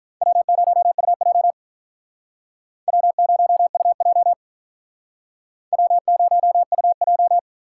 Morse code, heard as a keyed tone, about 42 wpm, 700 Hz. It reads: W0UJ W0UJ W0UJ
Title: Speed Racing - USA Call Signs 35wpm